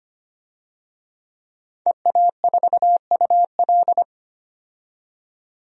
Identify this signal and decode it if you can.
Morse code, heard as a keyed tone, about 25 wpm, 700 Hz. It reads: EA4UL
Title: DXCC-100 Call Signs - 5 Characters - 25wpm